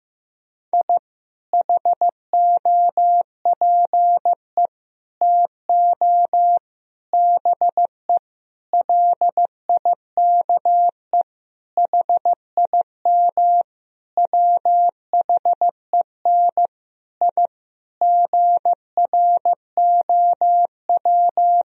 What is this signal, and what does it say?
Morse code, heard as a keyed tone, about 14 wpm, 700 Hz. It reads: I HOPE TO BE LIKE HIM WHEN I GROW
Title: Sentences from Top 1000 Words - Review Part 1 - 15wpm